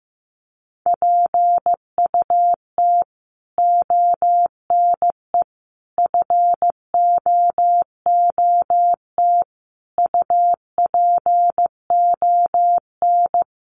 Morse code, heard as a keyed tone, about 15 wpm, 700 Hz. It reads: PUT ONE FOOT UPON
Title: Sentences from Top 1100 Words 15wpm